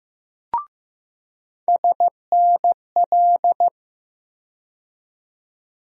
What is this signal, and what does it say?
Morse code, heard as a keyed tone, about 15 wpm, 700 Hz. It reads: SNL
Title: USA Call Sign Postfixes - 3 Letters 15wpm